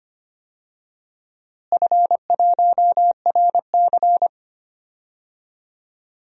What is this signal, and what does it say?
Morse code, heard as a keyed tone, about 25 wpm, 700 Hz. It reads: F1RC
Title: DXCC-100 Call Signs - 4 Characters - Repeated 3x - 25wpm